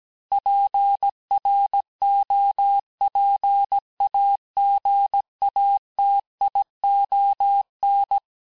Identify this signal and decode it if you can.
Morse code, heard as a keyed tone, about 17 wpm, 800 Hz. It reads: PROPAGATION
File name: corrige-cours-1581-fin-du-QSO-vitesse-16-mpm